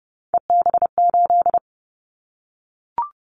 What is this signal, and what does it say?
Morse code, heard as a keyed tone, about 30 wpm, 700 Hz. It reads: EB8
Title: Callsign Prefixes for Top 100 DXCC - 30wpm